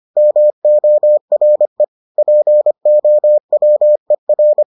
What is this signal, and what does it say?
Morse code, heard as a keyed tone, about 25 wpm, 600 Hz. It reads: MORE POWER